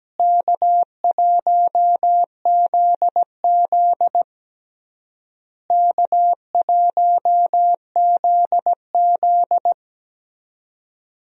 Morse code, heard as a keyed tone, about 17 wpm, 700 Hz. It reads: K1ZZ K1ZZ
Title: USA Call Signs - 4 Characters - Repeated 3x - 17wpm